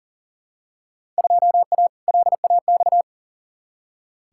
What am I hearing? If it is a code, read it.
Morse code, heard as a keyed tone, about 40 wpm, 700 Hz. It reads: 2A LAX
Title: Field Day Exchanges - 40wpm